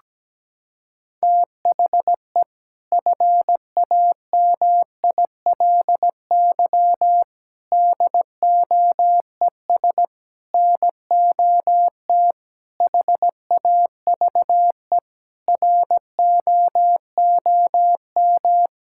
Morse code, heard as a keyed tone, about 17 wpm, 700 Hz. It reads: THE FAMILY DOES NOT HAVE ROOM
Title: Sentences from Top 1000 Words - Review Part 5 - 17wpm